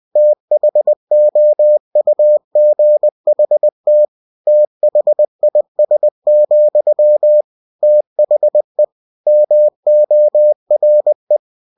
Morse code, heard as a keyed tone, about 20 wpm, 600 Hz. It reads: THOUGHT THIS, THE MORE
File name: cw-books / winnie-the-pooh / winnie-20wpm-page-0099